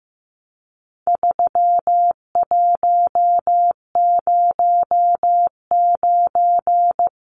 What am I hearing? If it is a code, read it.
Morse code, heard as a keyed tone, about 15 wpm, 700 Hz. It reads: 3109
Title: Numbers - 4-Digit 15wpm